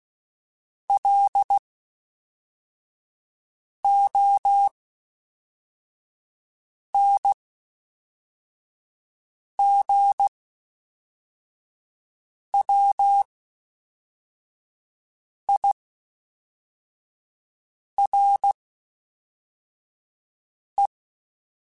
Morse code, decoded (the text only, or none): LONGWIRE